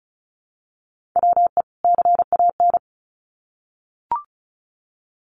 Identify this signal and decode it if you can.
Morse code, heard as a keyed tone, about 34 wpm, 700 Hz. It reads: WE CAN
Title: Sets of 2 Words 35wpm